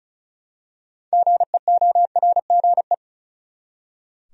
Morse code, heard as a keyed tone, about 34 wpm, 700 Hz. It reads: GEORGE
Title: USA Common Names - 35wpm